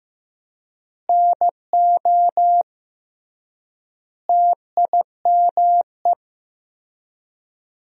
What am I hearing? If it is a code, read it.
Morse code, heard as a keyed tone, about 15 wpm, 700 Hz. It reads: NO TIME
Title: 3x Word-Spacing - Sentences from Top 400 Words - 15wpm